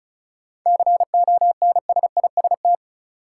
Morse code, heard as a keyed tone, about 34 wpm, 700 Hz. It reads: CONSIST
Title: Sentences from Top 1400 Words - New Word Drill Alt - 35wpm